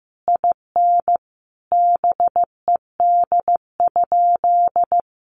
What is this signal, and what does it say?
Morse code, heard as a keyed tone, about 15 wpm, 700 Hz. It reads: IN BED?
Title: Sentences from Top 1200 Words 15wpm